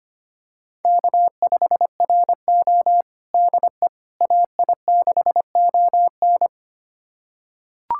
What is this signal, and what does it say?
Morse code, heard as a keyed tone, about 25 wpm, 700 Hz. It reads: K5RO DE AI6ON
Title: USA Callsign DE Callsign 25wpm